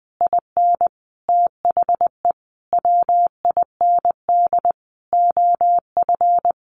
Morse code, heard as a keyed tone, about 20 wpm, 700 Hz. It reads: IN THE WIND OF
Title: A Sound Of Thunder 20wpm